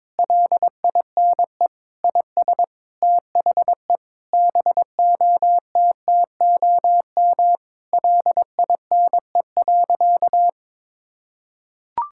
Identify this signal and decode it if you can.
Morse code, heard as a keyed tone, about 22 wpm, 700 Hz. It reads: LINE IS THE BOTTOM LINE.
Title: Sentences from Top 1000 Words - Review Part 6 - 22wpm